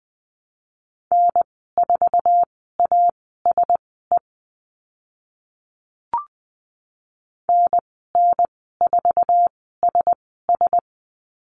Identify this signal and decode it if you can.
Morse code, heard as a keyed tone, about 20 wpm, spaced as for 14 wpm, 700 Hz. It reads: N4ASE NN4SS
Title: 048 Lesson 9g - Number 4 - Call Signs 20wpm